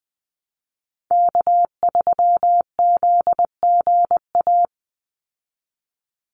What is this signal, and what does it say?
Morse code, heard as a keyed tone, about 20 wpm, 700 Hz. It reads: K3ZGA
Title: Call Signs Repeated 3x - 20wpm